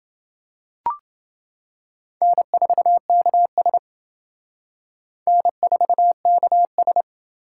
Morse code, heard as a keyed tone, about 28 wpm, 700 Hz. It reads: N4KS N4KS